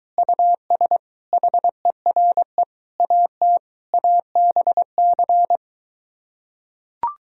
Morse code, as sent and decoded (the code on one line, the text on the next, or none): ..- ... / .... .-. / .- - / .- -... -.-.
US HERE AT ABC